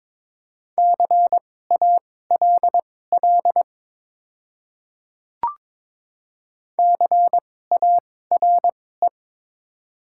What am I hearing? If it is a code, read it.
Morse code, heard as a keyed tone, about 22 wpm, spaced as for 16 wpm, 700 Hz. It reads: CALL CARE
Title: 2x Character Spacing - Top 300 Words - 22wpm